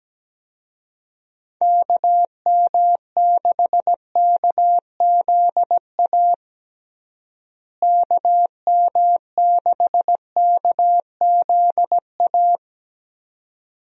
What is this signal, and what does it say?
Morse code, heard as a keyed tone, about 17 wpm, 700 Hz. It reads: KM6KZA KM6KZA
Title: USA Call Signs - 6 Characters - Repeated 2x - 17wpm